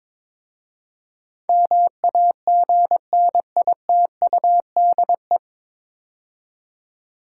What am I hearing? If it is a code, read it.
Morse code, heard as a keyed tone, about 22 wpm, 700 Hz. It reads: MAGNITUDE
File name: 9 Letter Words - First 3 Letters Given - 22wpm